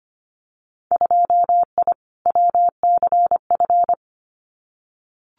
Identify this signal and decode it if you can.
Morse code, heard as a keyed tone, about 25 wpm, 700 Hz. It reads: 2I WCF
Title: Winter Field Day Exchanges 25wpm